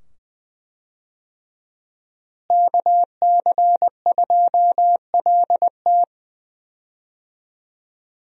Morse code, heard as a keyed tone, about 20 wpm, 700 Hz. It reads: KC2LT
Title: USA Call Signs - 5 Characters - Repeated 2x - 20wpm